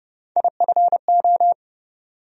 Morse code, heard as a keyed tone, about 30 wpm, 700 Hz. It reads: IFO
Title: USA Call Sign Postfixes - 3 Letters 30wpm